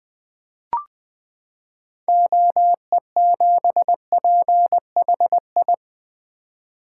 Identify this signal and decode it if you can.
Morse code, heard as a keyed tone, about 20 wpm, 700 Hz. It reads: OE7PHI
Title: 2023 SOTA Chaser Call Signs - 20wpm